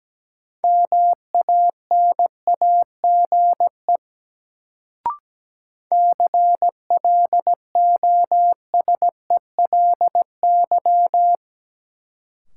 Morse code, heard as a keyed tone, about 17 wpm, 700 Hz. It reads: MANAGE CLOSELY
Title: Sentences from Top 1400 Words - New Word Drill Alt - 17wpm